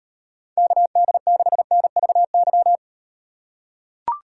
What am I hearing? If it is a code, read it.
Morse code, heard as a keyed tone, about 38 wpm, 700 Hz. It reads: KD6NVY